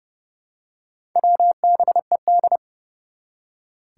Morse code, heard as a keyed tone, about 30 wpm, 700 Hz. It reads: WBED
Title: Four Challenging Letter-Number Combinations 30wpm